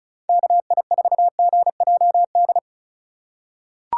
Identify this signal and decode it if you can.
Morse code, heard as a keyed tone, about 36 wpm, 700 Hz. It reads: KI4GJD
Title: USA Call Signs - 6 Characters - 4x Character Spacing - 35wpm